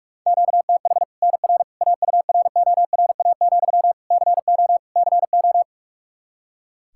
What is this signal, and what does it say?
Morse code, heard as a keyed tone, about 45 wpm, 700 Hz. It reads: QTH NR AURORA, CO CO
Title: Practice QSO - Exchange 1 - Beginning - 45wpm